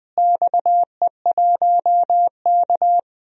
Morse code, heard as a keyed tone, about 21 wpm, 700 Hz. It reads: XE1K
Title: DXCC-100 Call Signs - 4 Characters - Repeated 2x - 20wpm